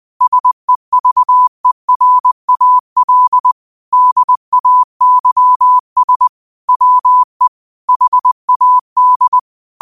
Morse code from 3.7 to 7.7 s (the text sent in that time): DAYS WE